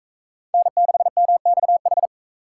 Morse code, heard as a keyed tone, about 43 wpm, 700 Hz. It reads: N6MXH